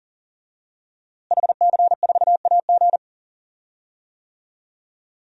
Morse code, heard as a keyed tone, about 40 wpm, 700 Hz. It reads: HC4AG
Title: DXCC-100 Call Signs - 5 Characters - Repeated 2x - 40wpm